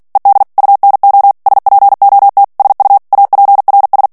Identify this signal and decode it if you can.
Morse code, heard as a keyed tone, about 48 wpm, 800 Hz. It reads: ED UNO SPOT SU APRS
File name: morse mp3 / ITALIEN texte clair 1 vitesse 45 mots minute